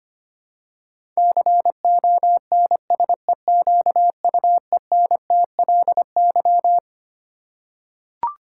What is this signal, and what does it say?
Morse code, heard as a keyed tone, about 25 wpm, 700 Hz. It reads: CONSEQUENTLY
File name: Twelve Letter Words - 25wpm